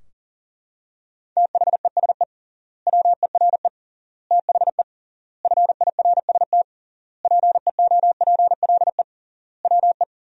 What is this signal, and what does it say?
Morse code, heard as a keyed tone, about 40 wpm, 700 Hz. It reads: THESE WERE THE FIRST PEOPLE WE
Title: 3x Word-Spacing - Sentences from Top 300 Words - 40wpm